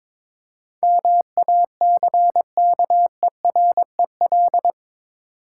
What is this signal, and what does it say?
Morse code, heard as a keyed tone, about 22 wpm, 690 Hz. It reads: MACKEREL
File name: Difficult Words with Hints - 22wpm